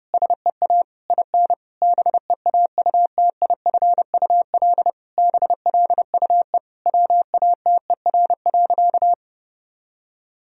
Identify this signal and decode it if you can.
Morse code, heard as a keyed tone, about 32 wpm, 700 Hz. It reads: SEA IN BEAUTIFUL BLUE WATER.
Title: Sentences from Top 1000 Words - Review Part 3 - 30wpm